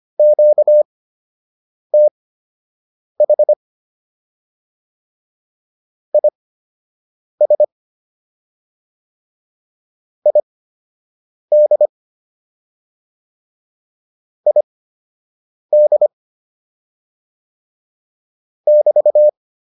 Morse code, heard as a keyed tone, about 25 wpm, spaced as for 7 wpm, 600 Hz. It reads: QTH IS ID ID <BT>